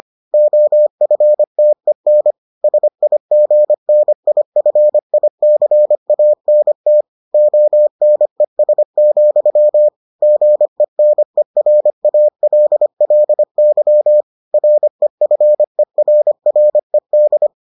Morse code, heard as a keyed tone, about 25 wpm, 600 Hz. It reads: OFTEN SIGNIFICANT ONES, GENERALLY REFERRED